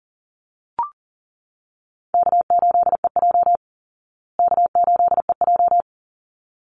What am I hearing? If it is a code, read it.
Morse code, heard as a keyed tone, about 40 wpm, 700 Hz. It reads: K8EJ K8EJ
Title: Easy Call Signs Repeated 2x - 40wpm